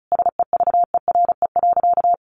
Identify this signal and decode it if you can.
Morse code, heard as a keyed tone, about 34 wpm, 700 Hz. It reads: SEVERE.
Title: Sentences from Top 1400 Words 35wpm